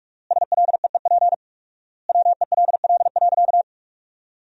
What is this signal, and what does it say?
Morse code, heard as a keyed tone, about 46 wpm, 690 Hz. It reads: SLEEP WELL.
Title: 4x Word-Spacing - Sentences from Top 500 Words - 45wpm